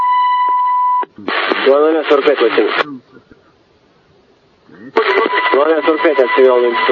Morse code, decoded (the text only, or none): none